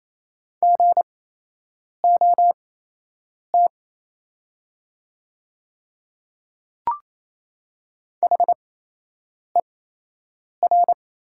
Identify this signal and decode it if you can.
Morse code, heard as a keyed tone, about 28 wpm, spaced as for 8 wpm, 700 Hz. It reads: GOT HER
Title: 8x Character Spacing - Top 500 Words - 28wpm